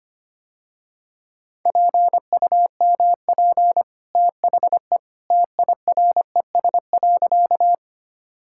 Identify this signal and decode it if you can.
Morse code, heard as a keyed tone, about 25 wpm, 700 Hz. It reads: PUMP THE TIRES.
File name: Easy Sentences with Hints - 25wpm